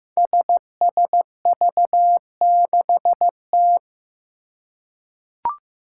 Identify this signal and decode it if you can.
Morse code, heard as a keyed tone, about 15 wpm, 700 Hz. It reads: SSV6T
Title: Five Challenging Letter-Number Combinations 15wpm